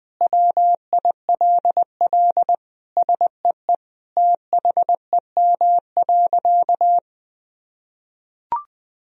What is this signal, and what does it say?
Morse code, heard as a keyed tone, about 20 wpm, 700 Hz. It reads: WILL SEE THEM.
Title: Sentences from Top 1000 Words - Review Part 2 - 20wpm